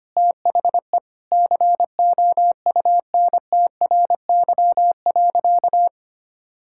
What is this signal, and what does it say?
Morse code, heard as a keyed tone, about 25 wpm, 700 Hz. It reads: THE COUNTRY.
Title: Sentences from Top 1000 Words - Review Part 5 - 25wpm